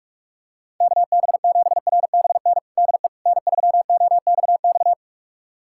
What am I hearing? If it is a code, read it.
Morse code, heard as a keyed tone, about 46 wpm, 700 Hz. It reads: KB7RBN DE N3OXX
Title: USA Callsign DE Callsign 45wpm